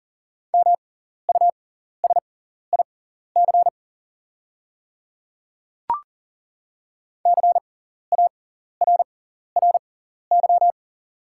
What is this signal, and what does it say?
Morse code, heard as a keyed tone, about 40 wpm, spaced as for 14 wpm, 700 Hz. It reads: MUSIC CARRY
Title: 6x Character Spacing - Top 300 Words - 40wpm